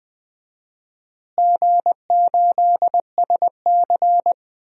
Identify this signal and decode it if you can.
Morse code, heard as a keyed tone, about 20 wpm, 700 Hz. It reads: G8SC
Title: DXCC-100 Call Signs - 4 Characters - Repeated 3x - 20wpm